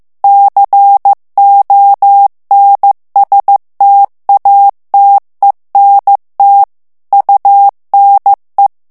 Morse code, heard as a keyed tone, about 15 wpm, 800 Hz. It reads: CONSTATENT UNE
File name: morse mp3 / FRANCAIS TEXTE CLAIR 1 VITESSE 14 MOTS MINUTES